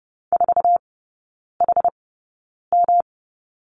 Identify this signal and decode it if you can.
Morse code, heard as a keyed tone, about 30 wpm, spaced as for 9 wpm, 700 Hz. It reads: 4HM
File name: Lesson 18e - Letter m - Three Characters 30wpm